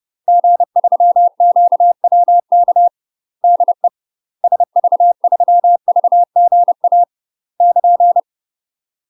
Morse code, heard as a keyed tone, about 30 wpm, 700 Hz. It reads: G3QWK DE SV3VGA <KN>
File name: UK MASTER QSOs TRAINING / 30WPM / UK MASTER QSOs 0041